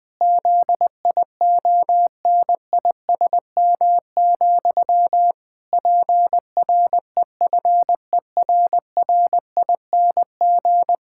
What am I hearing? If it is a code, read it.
Morse code, heard as a keyed tone, about 21 wpm, 700 Hz. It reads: ZIONISM, PREFERRING
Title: Pangram Challenge v2 - 20wpm